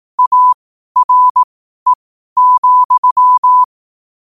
Morse code, heard as a keyed tone, about 18 wpm, spaced as for 12 wpm, 1000 Hz. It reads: ARE,